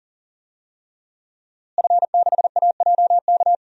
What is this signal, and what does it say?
Morse code, heard as a keyed tone, about 40 wpm, 700 Hz. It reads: F6AJK
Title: DXCC-100 Call Signs - 5 Characters - Repeated 3x - 40wpm